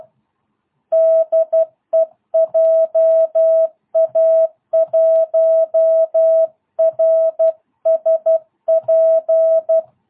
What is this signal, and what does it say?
Morse code, heard as a keyed tone, about 12 wpm, 660 Hz. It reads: DE JA1RSP